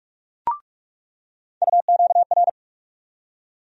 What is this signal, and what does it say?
Morse code, heard as a keyed tone, about 46 wpm, 700 Hz. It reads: UQR
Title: USA Call Sign Postfixes - 3 Letters 45wpm